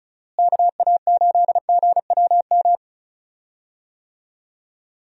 Morse code, heard as a keyed tone, about 36 wpm, 700 Hz. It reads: KA8GWM
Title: USA Call Signs - 6 Characters - Repeated 2x - 35wpm